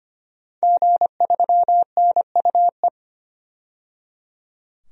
Morse code, heard as a keyed tone, about 25 wpm, 700 Hz. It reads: G3NUE
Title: DXCC-100 Call Signs - 5 Characters - 25wpm